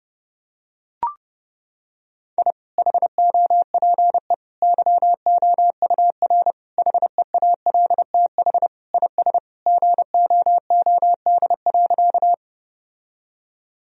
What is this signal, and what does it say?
Morse code, heard as a keyed tone, about 32 wpm, 700 Hz. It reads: I HOPE YOUR HEALTH IS GOOD.
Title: Sentences from Top 1000 Words - Review Part 3 - 30wpm